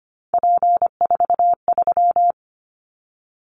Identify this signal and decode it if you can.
Morse code, heard as a keyed tone, about 25 wpm, 700 Hz. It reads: P43
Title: Callsign Prefixes for Top 100 DXCC - 25wpm